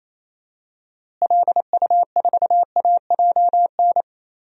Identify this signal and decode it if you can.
Morse code, heard as a keyed tone, about 28 wpm, 700 Hz. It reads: LU4AJN
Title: DXCC-100 Call Signs - 6 Characters - Repeated 2x - 28wpm